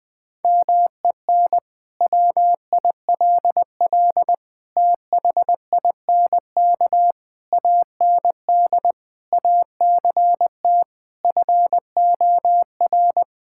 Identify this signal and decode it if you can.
Morse code, heard as a keyed tone, about 20 wpm, 700 Hz. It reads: MEN WILL THINK AND ACT FOR
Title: Sentences from Top 1000 Words - Review Part 5 - 20wpm